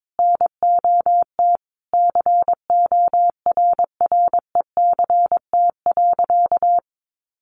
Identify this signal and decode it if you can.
Morse code, heard as a keyed tone, about 22 wpm, 700 Hz. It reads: NOT CORRECT.
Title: Sentences from Top 1200 Words 22wpm